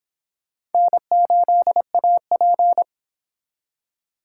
Morse code, heard as a keyed tone, about 30 wpm, 700 Hz. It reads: N8AP